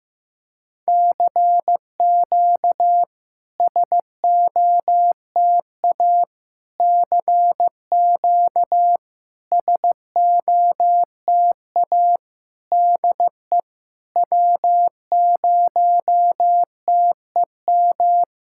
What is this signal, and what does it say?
Morse code, heard as a keyed tone, about 16 wpm, 700 Hz. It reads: CQ SOTA CQ SOTA DE W0TEM
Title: SOTA QSO Practice 15wpm